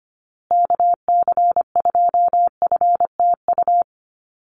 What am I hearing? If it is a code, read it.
Morse code, heard as a keyed tone, about 25 wpm, 700 Hz. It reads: KC2FTU
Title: USA Call Signs Repeated 2x 25wpm